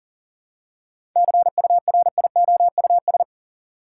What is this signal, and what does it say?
Morse code, heard as a keyed tone, about 40 wpm, 700 Hz. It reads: CURIOUS
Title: Seven Letter Words - 40wpm